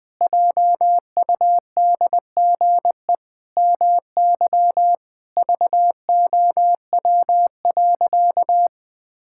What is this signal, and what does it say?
Morse code, heard as a keyed tone, about 20 wpm, 700 Hz. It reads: JUDGE MY VOW.
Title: Pangram Challenge v2 - 20wpm